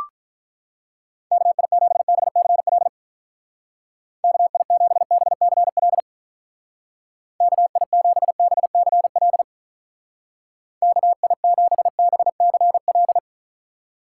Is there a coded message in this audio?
Yes: Morse code from 1.3 to 13.2 s, heard as a keyed tone, about 45 wpm, 700 Hz. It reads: KI7BCL KI7BCL KI7BCL KI7BCL